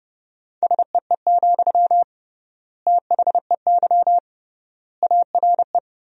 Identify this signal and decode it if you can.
Morse code, heard as a keyed tone, about 32 wpm, 700 Hz. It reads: SEE, THEY ARE
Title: 3x Word-Spacing - Sentences from Top 200 Words - 30wpm